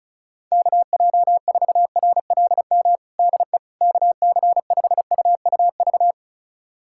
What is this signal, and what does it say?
Morse code, heard as a keyed tone, about 34 wpm, 700 Hz. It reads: KJ4RLM DE KC5UUV
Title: USA Callsign DE Callsign - 6 Characters - 35wpm